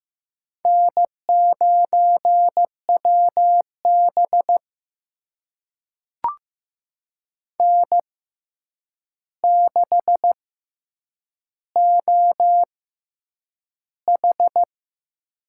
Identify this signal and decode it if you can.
Morse code, heard as a keyed tone, about 15 wpm, spaced as for 8 wpm, 700 Hz. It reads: N9WB N6OH
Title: USA Call Signs - 4 Characters - 6x Character Spacing - 15wpm